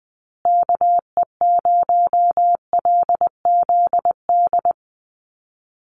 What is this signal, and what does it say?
Morse code, heard as a keyed tone, about 20 wpm, 700 Hz. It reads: KE0LZD
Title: USA Call Signs Repeated 2x 20wpm